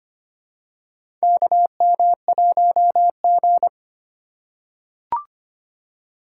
Morse code, heard as a keyed tone, about 25 wpm, 700 Hz. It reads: KM1G
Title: USA Call Signs - 4 Characters - 4x Character Spacing - 25wpm